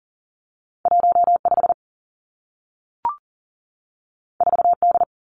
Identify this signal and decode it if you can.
Morse code, heard as a keyed tone, about 41 wpm, 700 Hz. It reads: 15 4D
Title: Two Challenging Letter-Number Combinations 40wpm